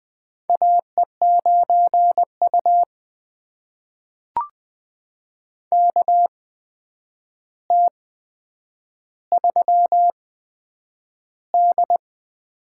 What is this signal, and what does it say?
Morse code, heard as a keyed tone, about 20 wpm, spaced as for 9 wpm, 700 Hz. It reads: AE9U KT3D
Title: USA Call Signs - 4 Characters - 8x Character Spacing - 20wpm